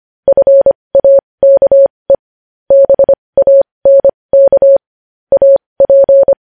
Morse code, heard as a keyed tone, about 25 wpm, spaced as for 20 wpm, 550 Hz. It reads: FAKE BANK AP
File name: podcast / News headlines-1x-25x20